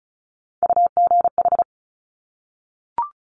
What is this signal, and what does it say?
Morse code, heard as a keyed tone, about 36 wpm, 700 Hz. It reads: UGH
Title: Top Three Letter Combinations in Words 35wpm